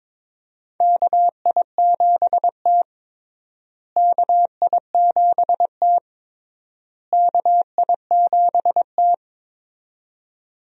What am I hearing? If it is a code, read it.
Morse code, heard as a keyed tone, about 22 wpm, 700 Hz. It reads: KI7T KI7T KI7T
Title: USA Call Signs - 4 Characters - Repeated 3x - 22wpm